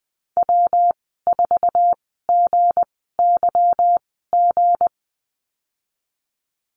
Morse code, heard as a keyed tone, about 19 wpm, spaced as for 14 wpm, 700 Hz. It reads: W4GYG
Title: 186 Lesson 30g - Letter b - Call Signs 20wpm